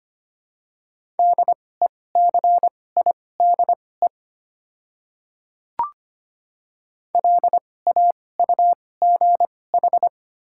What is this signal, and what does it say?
Morse code, heard as a keyed tone, about 25 wpm, spaced as for 18 wpm, 700 Hz. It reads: DECIDE LAUGH
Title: 2x Character Spacing - Top 500 Words - 25wpm